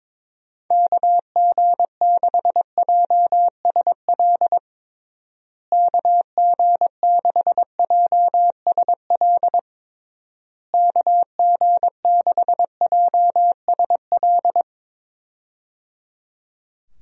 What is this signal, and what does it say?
Morse code, heard as a keyed tone, about 22 wpm, 700 Hz. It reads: KG6JSL KG6JSL KG6JSL
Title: USA Call Signs - 6 Characters - Repeated 3x - 22wpm